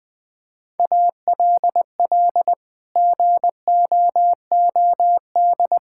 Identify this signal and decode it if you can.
Morse code, heard as a keyed tone, about 20 wpm, 700 Hz. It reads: ALL GOOD
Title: Sentences with 8-Word Phrases - Spoken Beginning - 20wpm